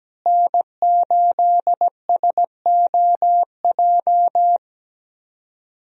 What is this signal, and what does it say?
Morse code, heard as a keyed tone, about 17 wpm, 700 Hz. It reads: N8SOJ
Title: USA Call Signs - 5 Characters - Repeated 2x - 17wpm